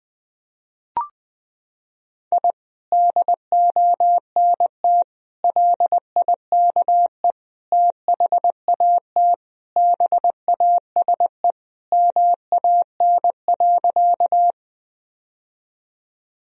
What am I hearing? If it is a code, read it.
Morse code, heard as a keyed tone, about 20 wpm, 700 Hz. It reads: I DONT LIKE THAT BASE MAN.
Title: Sentences from Top 1000 Words - Review Part 1 - 20wpm